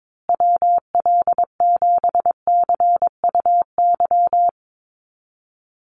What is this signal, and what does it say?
Morse code, heard as a keyed tone, about 22 wpm, 700 Hz. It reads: WL7CUY
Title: USA Call Signs Repeated 2x 22wpm